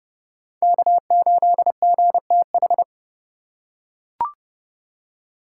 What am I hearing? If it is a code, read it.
Morse code, heard as a keyed tone, about 30 wpm, 700 Hz. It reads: K8GTH